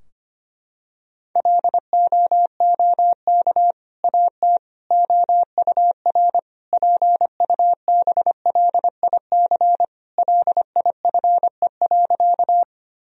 Morse code, heard as a keyed tone, about 25 wpm, 700 Hz. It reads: LOOK AT OUR PUBLIC LIFE.
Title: Sentences from Top 1000 Words - Review Part 3 - 25wpm